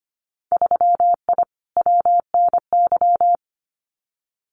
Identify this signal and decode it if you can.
Morse code, heard as a keyed tone, about 25 wpm, 700 Hz. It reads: 3I WNY
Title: Winter Field Day Exchanges Encore 25wpm